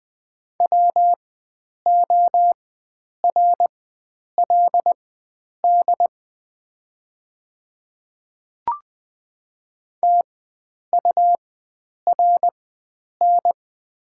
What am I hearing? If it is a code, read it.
Morse code, heard as a keyed tone, about 20 wpm, spaced as for 9 wpm, 700 Hz. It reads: WORLD TURN
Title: 4x Character Spacing - Top 300 Words - 20wpm